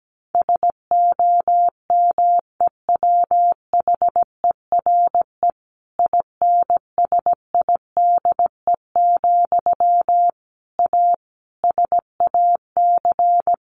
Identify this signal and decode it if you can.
Morse code, heard as a keyed tone, about 17 wpm, 700 Hz. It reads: SOMEWHERE INSIDE, A SAC
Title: A Sound Of Thunder 17wpm